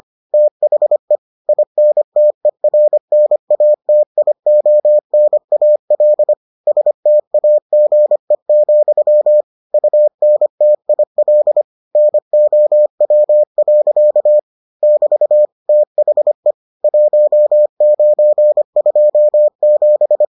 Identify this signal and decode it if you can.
Morse code, heard as a keyed tone, about 25 wpm, 600 Hz. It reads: THE INTERNATIONAL STAGE, UNTIL NOW. <BT> THE 1927